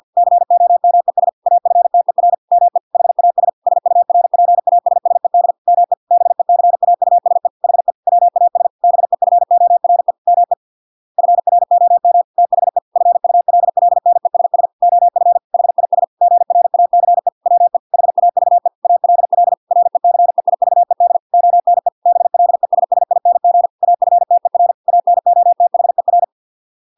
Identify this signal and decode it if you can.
Morse code, heard as a keyed tone, about 50 wpm, 700 Hz. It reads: COMES AFTER ME HAS SURPASSED ME BECAUSE HE WAS BEFORE ME FROM THE FULLNESS OF HIS GRACE WE HAVE ALL RECEIVED ONE BLESSING AFTER ANOTHER